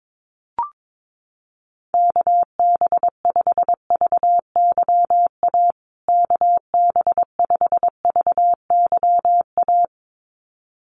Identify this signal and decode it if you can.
Morse code, heard as a keyed tone, about 22 wpm, 700 Hz. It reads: KB5VYA KB5VYA
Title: USA Call Signs Repeated 2x 22wpm